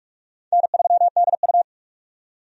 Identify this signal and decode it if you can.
Morse code, heard as a keyed tone, about 44 wpm, 700 Hz. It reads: N3DU
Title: USA Call Signs - 4 Characters - 45wpm